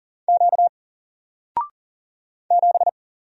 Morse code, heard as a keyed tone, about 40 wpm, 700 Hz. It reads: Q 7
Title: Single Character (Letter:Number:Punctuation:Prosign) 40wpm